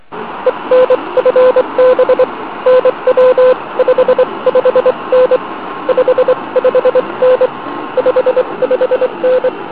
Morse code, heard as a keyed tone, about 23 wpm, 510 Hz, with noise. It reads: ENFB NW 55N 55N 55N